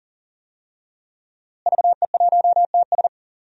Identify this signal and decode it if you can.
Morse code, heard as a keyed tone, about 40 wpm, 700 Hz. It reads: VE1TS
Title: DXCC-100 Call Signs - 5 Characters - Repeated 3x - 40wpm